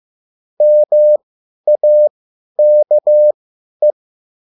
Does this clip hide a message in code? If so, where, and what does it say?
Yes: Morse code from 0.6 to 3.9 s, heard as a keyed tone, about 15 wpm, spaced as for 10 wpm, 600 Hz. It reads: MAKE